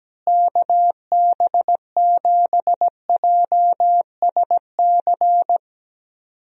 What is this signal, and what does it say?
Morse code, heard as a keyed tone, about 17 wpm, 700 Hz. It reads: KB7JSC
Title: USA Call Signs - 6 Characters - Repeated 2x - 17wpm